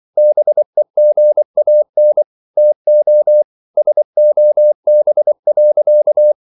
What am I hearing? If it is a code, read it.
Morse code, heard as a keyed tone, about 24 wpm, 600 Hz. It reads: BEGAN TO SOB.